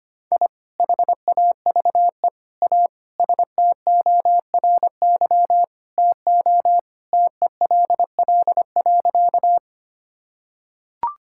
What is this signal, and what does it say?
Morse code, heard as a keyed tone, about 25 wpm, 700 Hz. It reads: I HAVE A STORY TO TELL.
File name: practice / 25wpm / Sentences with 2-Word Phrases - 25wpm